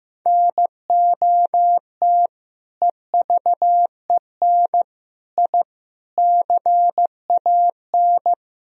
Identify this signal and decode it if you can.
Morse code, heard as a keyed tone, about 16 wpm, 700 Hz. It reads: NOT EVEN I CAN
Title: Sentences from Top 1000 Words - Review Part 2 - 15wpm